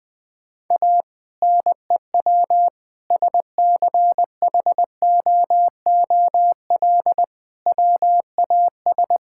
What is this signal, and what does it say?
Morse code, heard as a keyed tone, about 20 wpm, 700 Hz. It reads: A NEW SCHOOL WAS
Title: Sentences with 2-Word Phrases - Spoken Beginning - 20wpm